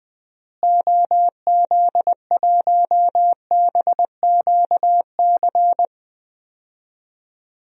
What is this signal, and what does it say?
Morse code, heard as a keyed tone, about 20 wpm, 700 Hz. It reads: OZ1BQC
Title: DXCC-100 Call Signs - 6 Characters - Repeated 2x - 20wpm